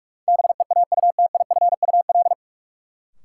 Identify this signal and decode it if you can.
Morse code, heard as a keyed tone, about 42 wpm, 700 Hz. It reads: BEAUTIFUL
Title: Sentences from Top 900 Words - New Word Drill - 45wpm